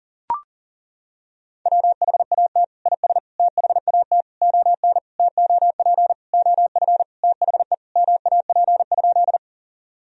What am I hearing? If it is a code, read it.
Morse code, heard as a keyed tone, about 40 wpm, 700 Hz. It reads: WHAT IS THAT ON TOP OF THE MAP?
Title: Sentences from Top 1000 Words - Review Part 2 - 40wpm